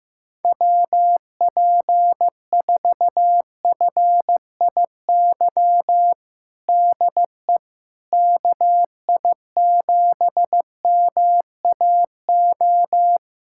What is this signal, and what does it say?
Morse code, heard as a keyed tone, about 15 wpm, 700 Hz. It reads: WP4FIY DE KI7MAO
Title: USA Callsign DE Callsign 15wpm